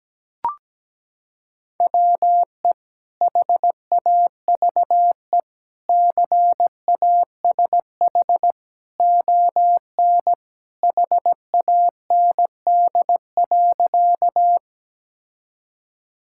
Morse code, heard as a keyed tone, about 17 wpm, 700 Hz. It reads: WE HAVE CASH ON HAND.
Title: Sentences from Top 1000 Words - Review Part 5 - 17wpm